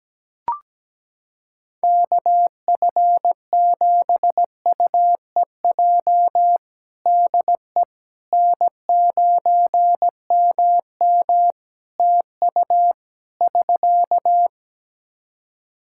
Morse code, heard as a keyed tone, about 17 wpm, 700 Hz. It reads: KF7UEJ DE N9MM TU <SK>
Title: Practice QSO - Exchange 4 - Ending - 17wpm